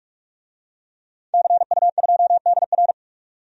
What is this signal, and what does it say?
Morse code, heard as a keyed tone, about 45 wpm, 700 Hz. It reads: CU2DR
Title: DXCC-100 Call Signs - 5 Characters - 45wpm